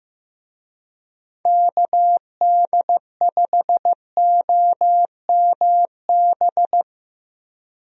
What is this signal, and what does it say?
Morse code, heard as a keyed tone, about 15 wpm, 700 Hz. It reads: KD5OMB
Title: USA Call Signs - 6 Characters - Repeated 3x - 15wpm